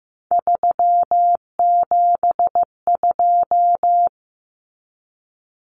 Morse code, heard as a keyed tone, about 15 wpm, 700 Hz. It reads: 372
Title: Common Cut Numbers - 3-Digit 15wpm